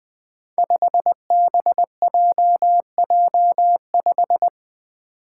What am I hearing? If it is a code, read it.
Morse code, heard as a keyed tone, about 20 wpm, 700 Hz. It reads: 5BJJ5
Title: Five Challenging Letter-Number Combinations 20wpm